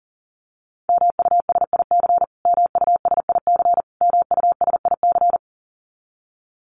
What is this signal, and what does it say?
Morse code, heard as a keyed tone, about 40 wpm, 700 Hz. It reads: MUSIC MUSIC MUSIC
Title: Top 500 Words Repeated 40wpm